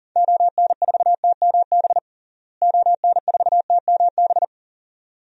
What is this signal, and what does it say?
Morse code, heard as a keyed tone, about 40 wpm, 700 Hz. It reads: ON4TMB ON4TMB
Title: DXCC-100 Call Signs - 6 Characters - Repeated 2x - 40wpm